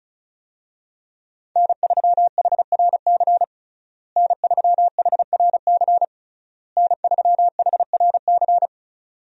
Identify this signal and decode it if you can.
Morse code, heard as a keyed tone, about 36 wpm, 700 Hz. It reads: N3HRC N3HRC N3HRC
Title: USA Call Signs - 5 Characters - Repeated 3x - 35wpm